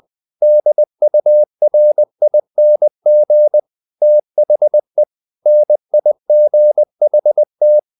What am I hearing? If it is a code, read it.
Morse code, heard as a keyed tone, about 20 wpm, 600 Hz. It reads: DURING THE NIGHT
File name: the old man and the sea / the old man and the sea part 1 20wpm